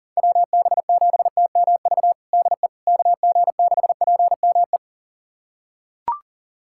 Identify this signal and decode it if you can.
Morse code, heard as a keyed tone, about 40 wpm, 700 Hz. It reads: WB7TMV DE KG6PME
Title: USA Callsign DE Callsign 40wpm